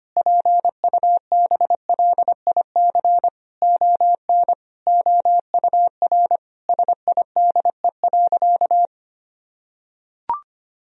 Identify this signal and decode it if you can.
Morse code, heard as a keyed tone, about 25 wpm, 700 Hz. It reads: PUBLIC ON OUR SIDE.
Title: Sentences from Top 1000 Words - Review Part 3 - 25wpm